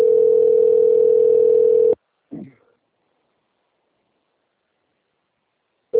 Ringback tone (North America), 440 + 480 Hz.